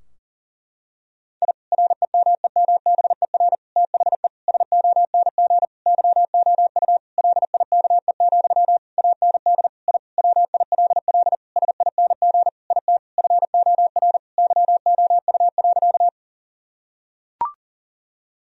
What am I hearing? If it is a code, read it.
Morse code, heard as a keyed tone, about 40 wpm, 700 Hz. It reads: I REMEMBER THE SONG YOU LIKE, AND I WILL SING IT FOR YOU.
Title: Sentences from Top 1000 Words - Review Part 2 - 40wpm